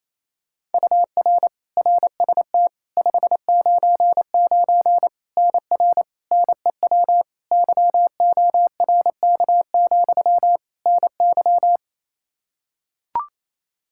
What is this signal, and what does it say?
Morse code, heard as a keyed tone, about 28 wpm, 700 Hz. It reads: UR RST 599 NR NEW YORK, NY
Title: K1USN SST QSO Practice 28wpm